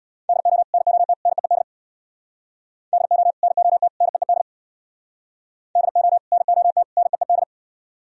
Morse code, heard as a keyed tone, about 64 wpm, 700 Hz. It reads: DO NOT NEED DO NOT NEED DO NOT NEED